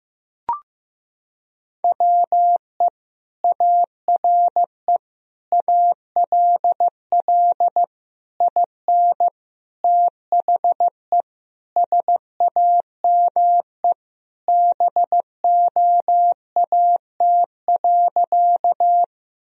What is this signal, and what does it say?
Morse code, heard as a keyed tone, about 16 wpm, 700 Hz. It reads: WE ARE ALL IN THE SAME BOAT.
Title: Sentences from Top 1000 Words - Review Part 3 - 15wpm